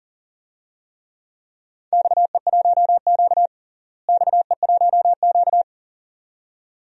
Morse code, heard as a keyed tone, about 40 wpm, 700 Hz. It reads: XE1Q XE1Q
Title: DXCC-100 Call Signs - 4 Characters - Repeated 2x - 40wpm